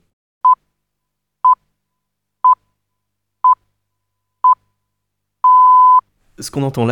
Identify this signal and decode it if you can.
Radio time signal (time pips), with speech. It is heard from 0.4 to 6.0 s.